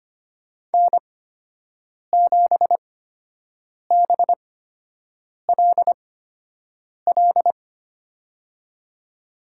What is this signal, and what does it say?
Morse code, heard as a keyed tone, about 25 wpm, spaced as for 7 wpm, 700 Hz. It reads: N7BLL
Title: USA Call Signs - 5 Characters - 8x Character Spacing - 25wpm